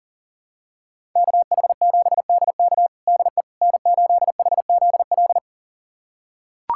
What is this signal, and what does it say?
Morse code, heard as a keyed tone, about 40 wpm, 700 Hz. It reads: KH7DK DE N8HZL
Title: USA Callsign DE Callsign - 5 Characters - 40wpm